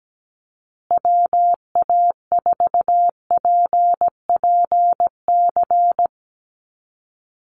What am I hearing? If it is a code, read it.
Morse code, heard as a keyed tone, about 17 wpm, 700 Hz. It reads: WA4PPC
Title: USA Call Signs Repeated 2x 17wpm